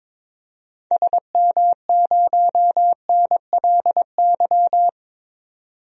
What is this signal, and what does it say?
Morse code, heard as a keyed tone, about 22 wpm, 700 Hz. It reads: SM0NLY
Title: DXCC-100 Call Signs - 6 Characters - 22wpm